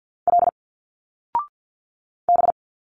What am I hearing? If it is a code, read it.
Morse code, heard as a keyed tone, about 51 wpm, 700 Hz. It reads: L B
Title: Single Letters 50wpm